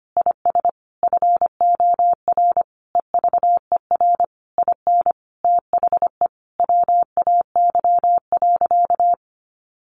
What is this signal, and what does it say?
Morse code, heard as a keyed tone, about 25 wpm, 700 Hz. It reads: IS FOR EVER IN THE WAY.
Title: Sentences from Top 400 Words 25wpm